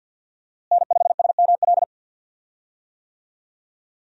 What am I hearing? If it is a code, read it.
Morse code, heard as a keyed tone, about 49 wpm, 700 Hz. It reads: N5SML
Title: USA Call Signs - 5 Characters - Repeated 2x - 50wpm